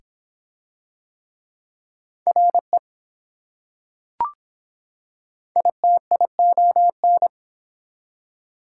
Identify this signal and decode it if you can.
Morse code, heard as a keyed tone, about 24 wpm, 700 Hz. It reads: RE ITION